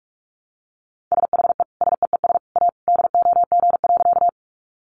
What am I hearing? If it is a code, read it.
Morse code, heard as a keyed tone, about 44 wpm, 700 Hz. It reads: SHE SEES A DOG.
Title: Easy Phrases 45wpm